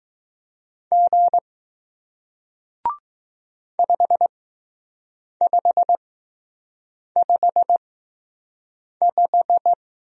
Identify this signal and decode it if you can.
Morse code, heard as a keyed tone, about 20 wpm, spaced as for 6 wpm, 700 Hz. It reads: G 5555